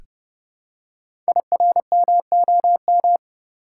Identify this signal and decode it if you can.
Morse code, heard as a keyed tone, about 30 wpm, 690 Hz. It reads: IRMOM
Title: Code Groups - 5 Letters - 30wpm